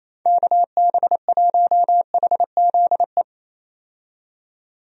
Morse code, heard as a keyed tone, about 28 wpm, 700 Hz. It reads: KB1HZE
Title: USA Call Signs - 6 Characters - 28wpm